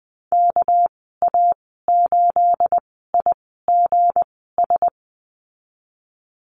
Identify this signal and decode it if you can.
Morse code, heard as a keyed tone, about 20 wpm, spaced as for 15 wpm, 700 Hz. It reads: KA8IGS
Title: 238 Lesson 38g - Letter q - Call Signs 20wpm